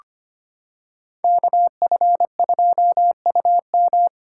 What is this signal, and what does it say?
Morse code, heard as a keyed tone, about 25 wpm, 700 Hz. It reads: KF2UM
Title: USA Callsign DE Callsign 25wpm